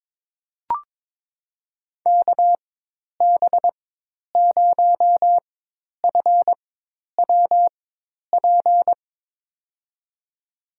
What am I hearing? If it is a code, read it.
Morse code, heard as a keyed tone, about 22 wpm, spaced as for 10 wpm, 700 Hz. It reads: KB0FWP
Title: USA Call Signs - 6 Characters - 4x Character Spacing - 22wpm